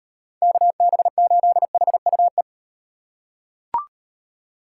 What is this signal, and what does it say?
Morse code, heard as a keyed tone, about 38 wpm, 700 Hz. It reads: KB8HUE